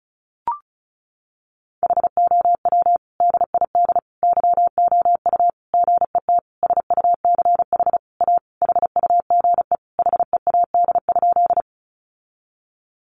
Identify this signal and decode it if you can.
Morse code, heard as a keyed tone, about 34 wpm, 700 Hz. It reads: HOW DID YOU GET SUCH A HUGE HEAD?
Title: Sentences from Top 1100 Words 35wpm